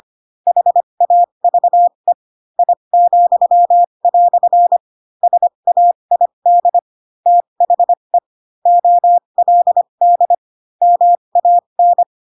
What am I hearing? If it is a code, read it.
Morse code, heard as a keyed tone, about 25 wpm, spaced as for 22 wpm, 700 Hz. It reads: HAVE I," SAID THE OLD MAN